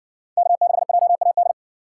Morse code, heard as a keyed tone, about 60 wpm, 700 Hz. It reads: K6JAD